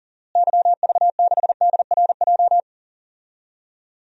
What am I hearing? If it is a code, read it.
Morse code, heard as a keyed tone, about 40 wpm, 700 Hz. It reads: YV6DRJ
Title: DXCC-100 Call Signs - 6 Characters - Repeated 2x - 40wpm